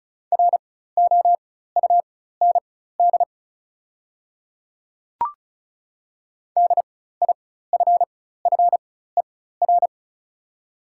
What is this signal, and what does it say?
Morse code, heard as a keyed tone, about 36 wpm, spaced as for 16 wpm, 690 Hz. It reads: ROUND DIFFER
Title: 4x Character Spacing - Top 300 Words - 35wpm